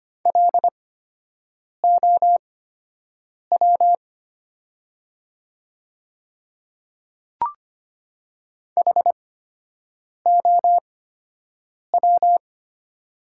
Morse code, heard as a keyed tone, about 25 wpm, spaced as for 7 wpm, 700 Hz. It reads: LOW HOW
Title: 8x Character Spacing - Top 300 Words - 25wpm